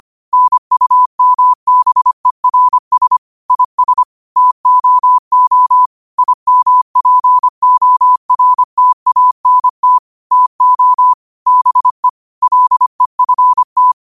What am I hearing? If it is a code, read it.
Morse code, heard as a keyed tone, about 25 wpm, 1000 Hz. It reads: NUMBERS IS TOO IMPORTANT TO BE LEFT
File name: qotdfiles / 25wpm